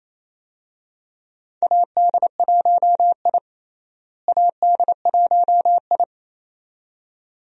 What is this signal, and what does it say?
Morse code, heard as a keyed tone, about 28 wpm, 700 Hz. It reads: AD1I AD1I
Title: USA Call Signs - 4 Characters - Repeated 2x - 28wpm